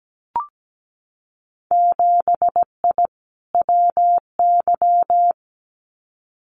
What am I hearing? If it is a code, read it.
Morse code, heard as a keyed tone, about 17 wpm, 700 Hz. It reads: 7I WY
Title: Winter Field Day Exchanges Encore 17wpm